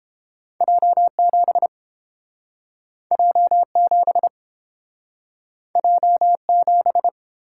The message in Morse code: J7 J7 J7

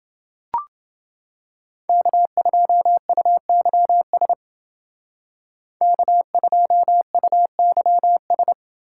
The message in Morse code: K2UYS K2UYS

